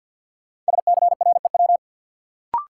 Morse code, heard as a keyed tone, about 53 wpm, 700 Hz. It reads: SCREW
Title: Sentences from Top 1600 Words - New Word Drill Alt - 50wpm